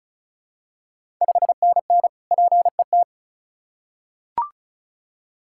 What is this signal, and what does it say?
Morse code, heard as a keyed tone, about 36 wpm, 690 Hz. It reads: 5NN PET
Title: Ontario QSO Party Exchange 35wpm